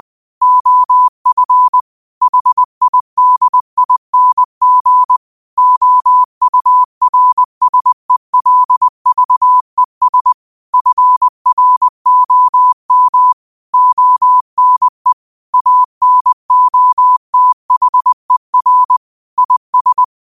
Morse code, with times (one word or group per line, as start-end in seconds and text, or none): 0.4-1.8 OF
2.2-5.2 HIDING
5.6-10.3 OURSELVES
10.7-13.3 FROM
13.7-15.1 ONE
15.5-19.0 ANOTHER
19.4-20.0 IS